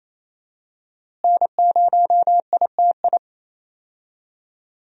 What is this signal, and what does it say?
Morse code, heard as a keyed tone, about 27 wpm, 700 Hz. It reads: N0ITI
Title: USA Call Signs - 5 Characters - Repeated 2x - 28wpm